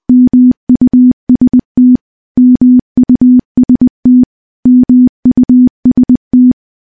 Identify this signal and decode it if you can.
Morse code, heard as a keyed tone, about 21 wpm, 250 Hz. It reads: MUST MUST MUST
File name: morse-training-20WPM